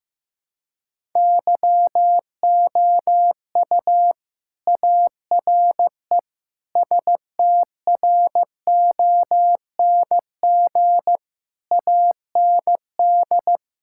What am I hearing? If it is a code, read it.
Morse code, heard as a keyed tone, about 15 wpm, 700 Hz. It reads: YOU ARE STRONG AND
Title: Sentences from Top 1000 Words - Review Part 5 - 15wpm